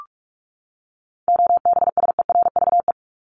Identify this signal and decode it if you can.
Morse code, heard as a keyed tone, about 44 wpm, 700 Hz. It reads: OBSERVE
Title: ED Suffix - Top 100 Words with Base Words 45wpm